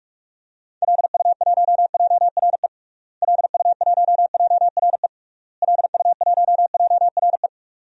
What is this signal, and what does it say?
Morse code, heard as a keyed tone, about 46 wpm, 700 Hz. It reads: LU1JRE LU1JRE LU1JRE
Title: DXCC-100 Call Signs - 6 Characters - Repeated 3x - 45wpm